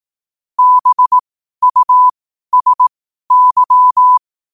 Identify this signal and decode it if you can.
Morse code, heard as a keyed tone, about 18 wpm, spaced as for 12 wpm, 1000 Hz. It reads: BUSY
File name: qotdfiles / 12wpm